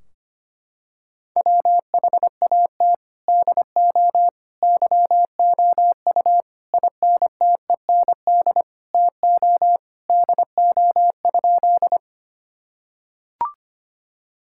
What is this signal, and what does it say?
Morse code, heard as a keyed tone, about 25 wpm, 700 Hz. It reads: WHAT DO YOU INTEND TO DO?
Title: Sentences from Top 1600 Words 25wpm